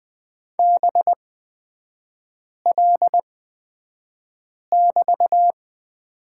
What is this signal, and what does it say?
Morse code, heard as a keyed tone, about 20 wpm, spaced as for 5 wpm, 700 Hz. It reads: BL<BT>
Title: 224 Lesson 36e - BT - Three Characters 20wpm